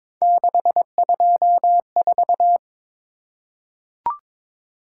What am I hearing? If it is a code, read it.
Morse code, heard as a keyed tone, about 22 wpm, 700 Hz. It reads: 624
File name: practice / 22wpm / Three Letter-Number Combinations - 22wpm